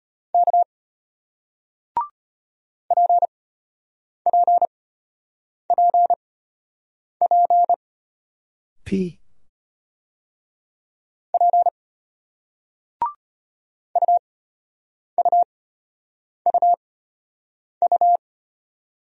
Morse code, heard as a keyed tone, about 36 wpm, spaced as for 8 wpm, 700 Hz. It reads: K PPPP P UUUU